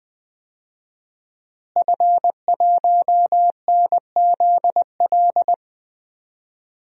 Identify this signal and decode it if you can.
Morse code, heard as a keyed tone, about 20 wpm, 700 Hz. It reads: F1NZL
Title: DXCC-100 Call Signs - 5 Characters - Repeated 3x - 20wpm